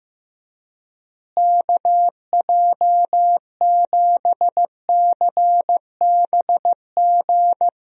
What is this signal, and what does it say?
Morse code, heard as a keyed tone, about 15 wpm, 700 Hz. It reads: KJ7CBG
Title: USA Call Signs - 6 Characters - Repeated 2x - 15wpm